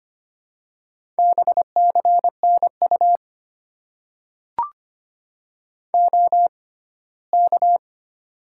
Morse code, heard as a keyed tone, about 25 wpm, spaced as for 17 wpm, 700 Hz. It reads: BCNU OK
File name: Top 90 QSO Elements - 6x Character Spacing - 25wpm